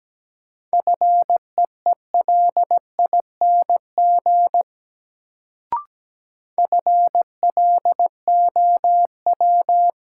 Morse code, heard as a keyed tone, about 17 wpm, 700 Hz. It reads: FEELING FLOW
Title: Sentences from Top 1200 Words - New Word Drill Alt - 17wpm